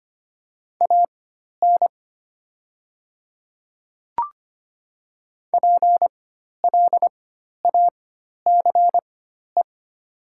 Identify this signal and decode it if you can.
Morse code, heard as a keyed tone, about 25 wpm, spaced as for 12 wpm, 700 Hz. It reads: AN PLACE
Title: 4x Character Spacing - Top 300 Words - 25wpm